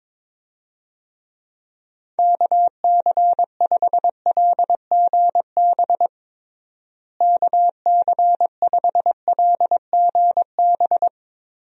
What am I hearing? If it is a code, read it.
Morse code, heard as a keyed tone, about 22 wpm, 700 Hz. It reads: KC5LGB KC5LGB
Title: USA Call Signs - 6 Characters - Repeated 3x - 22wpm